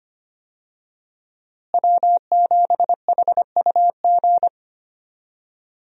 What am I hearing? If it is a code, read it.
Morse code, heard as a keyed tone, about 25 wpm, 700 Hz. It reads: W7HUG
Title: USA Call Signs - 5 Characters - Repeated 2x - 25wpm